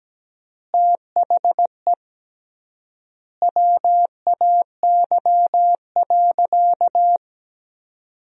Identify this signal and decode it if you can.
Morse code, heard as a keyed tone, about 17 wpm, 700 Hz. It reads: THE WAY.
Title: 3x Word-Spacing - Sentences from Top 100 Words - 17wpm